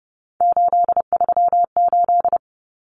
Morse code, heard as a keyed tone, about 30 wpm, 700 Hz. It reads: 838
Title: Numbers - 3-Digit 30wpm